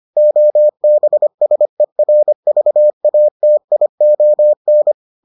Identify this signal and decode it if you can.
Morse code, heard as a keyed tone, about 25 wpm, 600 Hz. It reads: OBSERVATION